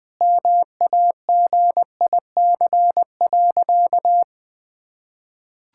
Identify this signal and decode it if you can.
Morse code, heard as a keyed tone, about 20 wpm, 700 Hz. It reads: MAGIC.
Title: Sentences with 4-Word Phrases - Spoken Beginning - 20wpm